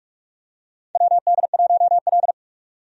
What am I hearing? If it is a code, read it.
Morse code, heard as a keyed tone, about 46 wpm, 700 Hz. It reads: WD1L
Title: USA Call Signs - 4 Characters - 45wpm